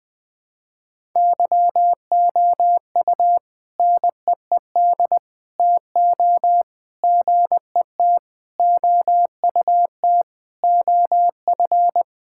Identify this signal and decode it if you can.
Morse code, heard as a keyed tone, about 20 wpm, 700 Hz. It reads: YOU NEED TO GET OUT OF
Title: Sentences with 4-Word Phrases - Spoken Beginning - 20wpm